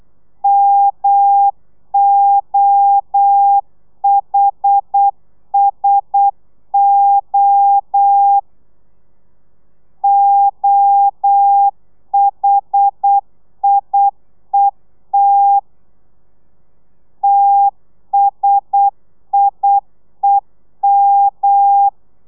Morse code, heard as a keyed tone, about 8 wpm, 800 Hz. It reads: MOHSO OHIET TSIEM
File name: telegrafia / i2tfj / Lez1 35